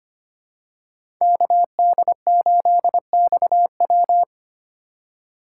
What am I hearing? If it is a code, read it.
Morse code, heard as a keyed tone, about 25 wpm, 700 Hz. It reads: KD8XW
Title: USA Call Signs - 5 Characters - 2x Character Spacing - 25wpm